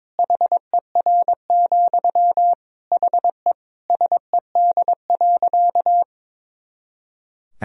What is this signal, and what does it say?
Morse code, heard as a keyed tone, about 23 wpm, 700 Hz. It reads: HER, HE SED.
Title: Sentences from Top 200 Words and Abbreviations 22wpm